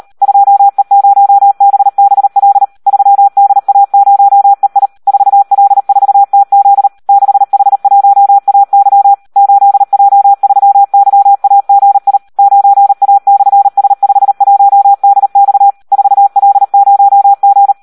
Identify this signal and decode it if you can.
Morse code, heard as a keyed tone, about 38 wpm, 800 Hz. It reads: 2E0BBL 3DA0EI 4L4TZ 6H1AY 8J3YAGI 9A/S51DX 4L0G
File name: Audio practica CW / SD 1 38